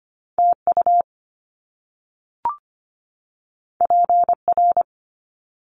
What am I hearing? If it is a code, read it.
Morse code, heard as a keyed tone, about 25 wpm, 700 Hz. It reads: TU PR
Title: Top Two Letter Combinations in Words 25wpm